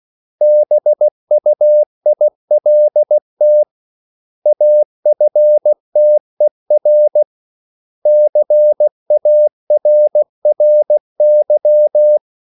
Morse code, heard as a keyed tone, about 16 wpm, 600 Hz. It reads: BUILT AFTER CARRY
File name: CW / Practice / 500MostCommonEnglishWords16